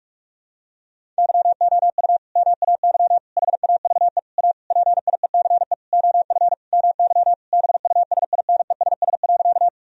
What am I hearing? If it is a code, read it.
Morse code, heard as a keyed tone, about 45 wpm, 700 Hz. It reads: YOU MAY HAVE A PIECE OF MY BUSINESS.
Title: Sentences from Top 1000 Words - Review Part 2 - 45wpm